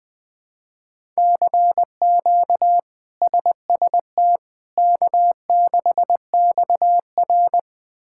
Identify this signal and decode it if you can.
Morse code, heard as a keyed tone, about 21 wpm, 700 Hz. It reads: CQ SST K6XR
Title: K1USN SST QSO Practice - 20wpm